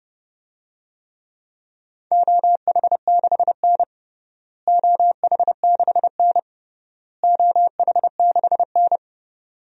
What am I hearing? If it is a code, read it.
Morse code, heard as a keyed tone, about 30 wpm, 700 Hz. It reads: OH6N OH6N OH6N
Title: DXCC-100 Call Signs - 4 Characters - Repeated 3x - 30wpm